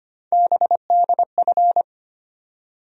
Morse code, heard as a keyed tone, about 25 wpm, 700 Hz. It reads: BDF
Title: USA Call Sign Postfixes - 3 Letters 25wpm